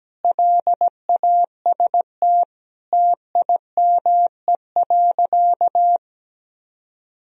Morse code, heard as a keyed tone, about 17 wpm, 700 Hz. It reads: LAST TIME.
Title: Sentences from Top 1000 Words - Review Part 3 - 17wpm